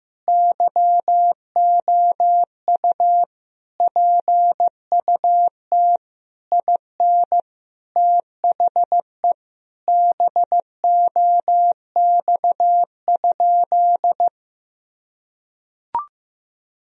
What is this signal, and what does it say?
Morse code, heard as a keyed tone, about 15 wpm, 700 Hz. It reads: YOU PUT IN THE BOX?
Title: Sentences from Top 1000 Words - Review Part 2 - 15wpm